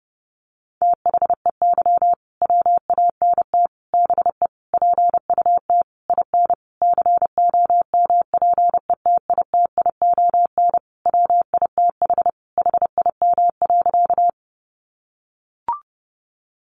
Morse code, heard as a keyed tone, about 30 wpm, 700 Hz. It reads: THEY WANT BE PUT IN COMPETITION WITH HIM.
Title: Sentences from Top 1000 Words 30wpm